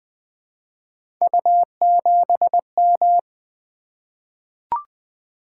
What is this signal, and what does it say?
Morse code, heard as a keyed tone, about 20 wpm, 700 Hz. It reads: U7M
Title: Three Letter-Number Combinations - 20wpm